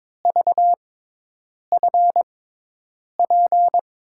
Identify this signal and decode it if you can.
Morse code, heard as a keyed tone, about 22 wpm, spaced as for 8 wpm, 700 Hz. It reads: VFP
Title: USA Call Signs - 6x Character Spacing - 22wpm